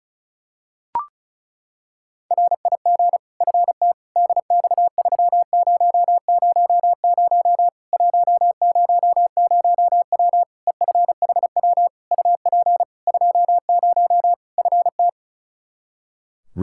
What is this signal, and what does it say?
Morse code, heard as a keyed tone, about 34 wpm, 700 Hz. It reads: RIG FT DX3000 100W EFHW UP 20 FT
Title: K1USN SST QSO Practice 35wpm